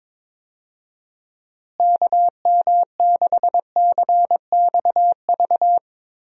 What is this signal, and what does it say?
Morse code, heard as a keyed tone, about 22 wpm, 700 Hz. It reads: KM6CXV
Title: USA Call Signs - 6 Characters - Repeated 3x - 22wpm